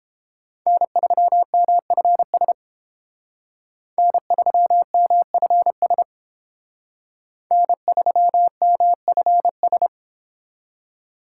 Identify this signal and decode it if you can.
Morse code, heard as a keyed tone, about 30 wpm, 700 Hz. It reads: N3MFS N3MFS N3MFS